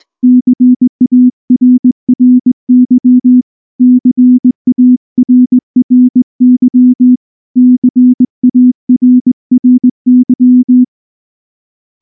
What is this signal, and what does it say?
Morse code, heard as a keyed tone, about 19 wpm, 250 Hz. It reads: CARRY CARRY CARRY
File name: morse / morse-training-20WPM-6